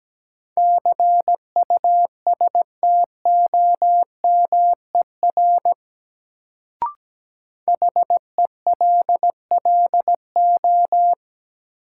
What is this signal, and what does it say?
Morse code, heard as a keyed tone, about 17 wpm, 700 Hz. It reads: CUSTOMER HELLO
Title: Sentences from Top 1100 Words - New Word Drill Alt - 17wpm